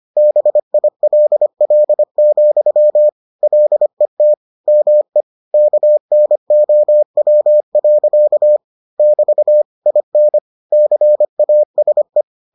Morse code, heard as a keyed tone, about 25 wpm, 600 Hz. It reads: BILL, LET ME KNOW. <BT> IN CASE